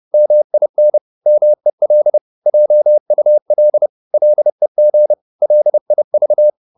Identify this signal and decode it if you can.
Morse code, heard as a keyed tone, about 30 wpm, 600 Hz. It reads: MIN MEL JUL LEG LIV
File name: Korte ord DK 30wpm